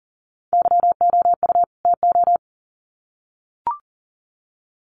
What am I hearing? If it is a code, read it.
Morse code, heard as a keyed tone, about 40 wpm, 700 Hz. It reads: YOU TO
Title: Sets of 2 Words 40wpm